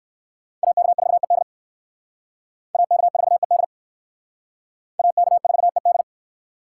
Morse code, heard as a keyed tone, about 56 wpm, 700 Hz. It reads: AK4ED AK4ED AK4ED
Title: Speed Racing - USA 2x2 Call Signs 45wpm